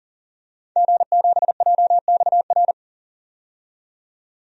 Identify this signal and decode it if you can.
Morse code, heard as a keyed tone, about 40 wpm, 700 Hz. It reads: G7JXR
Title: DXCC-100 Call Signs - 5 Characters - 40wpm